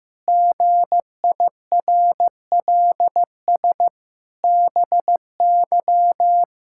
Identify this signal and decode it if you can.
Morse code, heard as a keyed tone, about 15 wpm, 700 Hz. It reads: GIRLS BY
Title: Pangram Challenge v2 - 15wpm